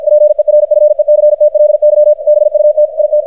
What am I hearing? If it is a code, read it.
Morse code, heard as a keyed tone, about 53 wpm, 600 Hz, with radio noise. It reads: PERFECTLY BUT IT
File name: archived / CRSnet